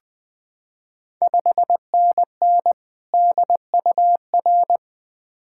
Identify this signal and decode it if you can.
Morse code, heard as a keyed tone, about 20 wpm, 700 Hz. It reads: 5NN DUR
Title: Ontario QSO Party Exchange 20wpm